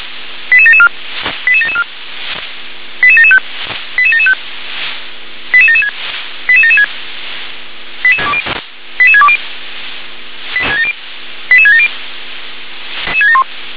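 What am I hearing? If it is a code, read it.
Five-tone selective call (ZVEI), heard as quick runs of five tones, tones of about 70 ms. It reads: 88774 88774 88775 88775 88776 88776 88620 88670
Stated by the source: Received by SDR as ZVEI five-tone selective call.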